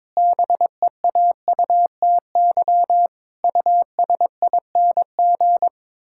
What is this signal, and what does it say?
Morse code, heard as a keyed tone, about 22 wpm, 700 Hz. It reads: BEAUTY USING
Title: Pangram Challenge v2 - 22wpm